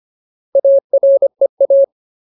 Morse code, heard as a keyed tone, about 25 wpm, 550 Hz. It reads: AREA
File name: Ultimate Word List / 25WPM Ultimate